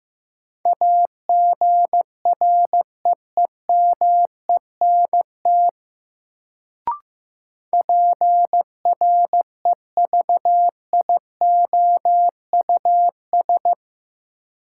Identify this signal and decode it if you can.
Morse code, heard as a keyed tone, about 15 wpm, 700 Hz. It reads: AGREEMENT PREVIOUS
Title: Sentences from Top 1500 Words - New Word Drill - 15wpm